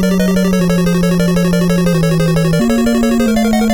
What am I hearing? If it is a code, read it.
There is no code.